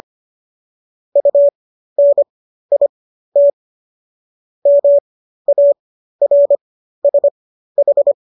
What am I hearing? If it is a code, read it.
Morse code, heard as a keyed tone, about 25 wpm, spaced as for 13 wpm, 570 Hz. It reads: UNIT MARSH